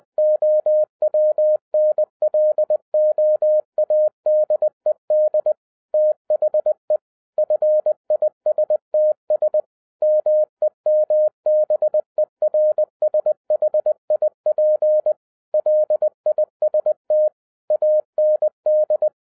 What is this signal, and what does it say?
Morse code, heard as a keyed tone, about 20 wpm, 610 Hz. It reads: OWNLOADED THE FISTS MEMBERSHIP LIST AND